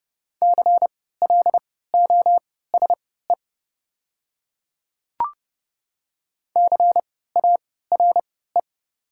Morse code, heard as a keyed tone, about 30 wpm, spaced as for 17 wpm, 700 Hz. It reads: CLOSE CARE
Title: 3x Character Spacing - Top 300 Words - 30wpm